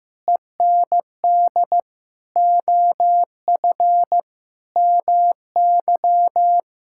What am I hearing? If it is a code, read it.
Morse code, heard as a keyed tone, about 16 wpm, 700 Hz. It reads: END OF MY
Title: Sentences from Top 1600 Words 15wpm